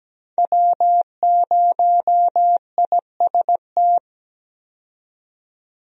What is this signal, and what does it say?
Morse code, heard as a keyed tone, about 17 wpm, 700 Hz. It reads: W0IST
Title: USA Call Signs - 5 Characters - 17wpm